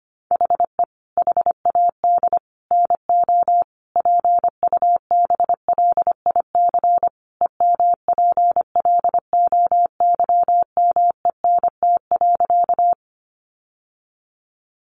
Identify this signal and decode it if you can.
Morse code, heard as a keyed tone, about 25 wpm, 700 Hz. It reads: HE HAD NO PUBLIC EMPLOYMENT.
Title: Sentences from Top 1500 Words 25wpm